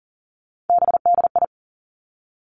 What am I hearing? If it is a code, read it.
Morse code, heard as a keyed tone, about 41 wpm, 700 Hz. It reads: BDI
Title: Three Challenging Letter-Number Combinations 40wpm